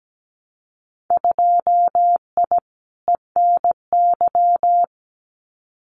Morse code, heard as a keyed tone, about 17 wpm, 700 Hz. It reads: 2I ENY
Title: Winter Field Day Exchanges Encore 17wpm